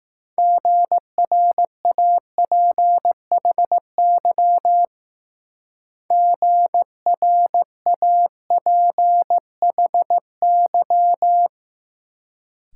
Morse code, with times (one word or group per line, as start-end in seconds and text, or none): 0.4-4.9 GRAPHY
6.1-11.5 GRAPHY